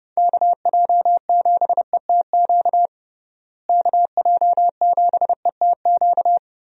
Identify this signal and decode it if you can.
Morse code, heard as a keyed tone, about 32 wpm, 700 Hz. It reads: KJ7ETQ KJ7ETQ
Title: USA Call Signs - 6 Characters - Repeated 2x - 30wpm